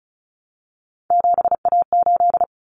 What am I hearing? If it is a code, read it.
Morse code, heard as a keyed tone, about 34 wpm, 700 Hz. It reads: 7A8
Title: Common Cut Numbers - 3-Digit 35wpm